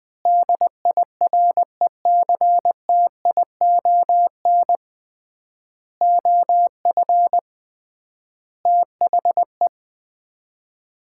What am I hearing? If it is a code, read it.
Morse code, heard as a keyed tone, about 21 wpm, 700 Hz. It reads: DIRECTION OF THE
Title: 3x Word-Spacing - Sets of 5 Words - 20wpm